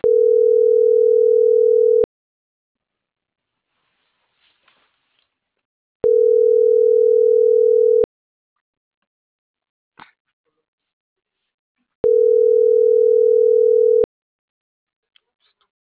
Ringback tone (North America), 440 + 480 Hz.